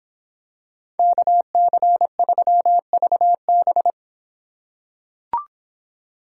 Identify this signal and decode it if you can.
Morse code, heard as a keyed tone, about 26 wpm, 700 Hz. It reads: KC3VB